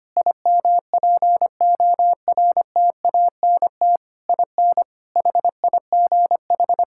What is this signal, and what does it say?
Morse code, heard as a keyed tone, about 25 wpm, 700 Hz. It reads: IMPORTANT IN HIGH
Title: Sentences from Top 1000 Words - Review Part 6 - 25wpm